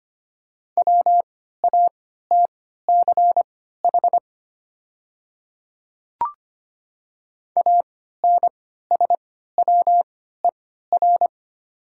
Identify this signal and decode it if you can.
Morse code, heard as a keyed tone, about 25 wpm, spaced as for 14 wpm, 700 Hz. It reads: WATCH ANSWER
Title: 3x Character Spacing - Top 300 Words - 25wpm